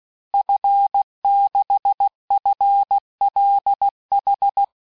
Morse code, heard as a keyed tone, about 16 wpm, 800 Hz. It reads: F6FLH
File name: cours-du-lundi files / cours-1580-fin-QSO